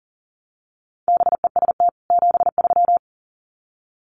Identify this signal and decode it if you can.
Morse code, heard as a keyed tone, about 40 wpm, 700 Hz. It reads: BEST 73
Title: Common QSO Phrases 40wpm